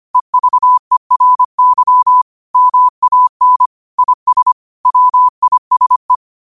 Morse code, heard as a keyed tone, about 25 wpm, 1000 Hz. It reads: EVERY MAN IS WISE